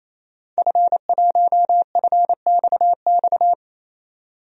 Morse code, heard as a keyed tone, about 28 wpm, 700 Hz. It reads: F1FXX
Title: DXCC-100 Call Signs - 5 Characters - 28wpm